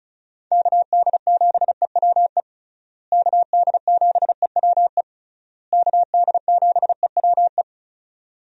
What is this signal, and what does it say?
Morse code, heard as a keyed tone, about 34 wpm, 700 Hz. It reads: KD7EWE KD7EWE KD7EWE
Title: USA Call Signs - 6 Characters - Repeated 3x - 35wpm